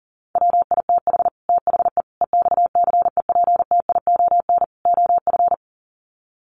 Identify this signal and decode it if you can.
Morse code, heard as a keyed tone, about 40 wpm, 700 Hz. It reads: WITH THE EXCEPTION OF
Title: Sets of 4 Words 40wpm